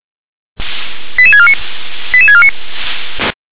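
Five-tone selective call (ZVEI), heard as quick runs of five tones, tones of about 80 ms. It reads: 88649 88649
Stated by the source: ZVEI five-tone selective call received on an SDR.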